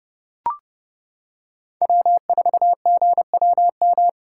Morse code, heard as a keyed tone, about 32 wpm, 700 Hz. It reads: W4GWM
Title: USA Callsign DE Callsign - 5 Characters - 30wpm